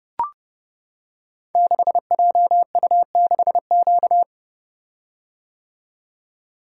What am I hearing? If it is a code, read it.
Morse code, heard as a keyed tone, about 30 wpm, 700 Hz. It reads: 6JU6Q
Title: Code Groups - 5 Letter:Numbers - 30wpm